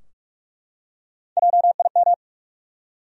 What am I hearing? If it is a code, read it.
Morse code, heard as a keyed tone, about 42 wpm, 700 Hz. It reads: JIM
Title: USA Common Names - Max 3 Letters - 45wpm